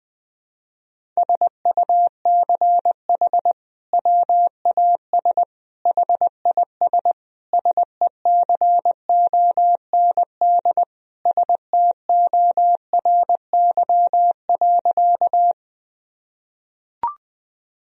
Morse code, heard as a keyed tone, about 20 wpm, 700 Hz. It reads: SUCH WAS HIS SECOND STORY.
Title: Sentences from Top 1000 Words - Review Part 1 - 20wpm